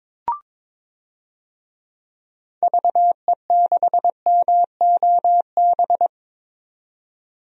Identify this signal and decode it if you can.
Morse code, heard as a keyed tone, about 22 wpm, 700 Hz. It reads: VE6MOB
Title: DXCC-100 Call Signs - 6 Characters - 22wpm